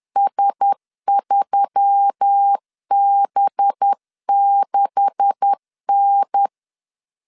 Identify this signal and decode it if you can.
Morse code, heard as a keyed tone, about 10 wpm, 800 Hz. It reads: S3B6N
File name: CW / DET / 1 9 1500C S B 10wpm 4439